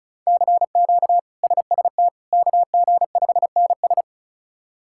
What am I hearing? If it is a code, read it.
Morse code, heard as a keyed tone, about 36 wpm, 700 Hz. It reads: CQ SST KG5NS
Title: K1USN SST QSO Practice - 35wpm